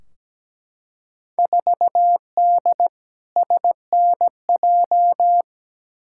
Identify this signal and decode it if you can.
Morse code, heard as a keyed tone, about 17 wpm, 700 Hz. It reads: 4D SNJ
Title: Field Day Exchanges Encore - 17wpm